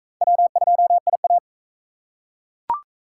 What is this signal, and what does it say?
Morse code, heard as a keyed tone, about 41 wpm, 700 Hz. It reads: W2IA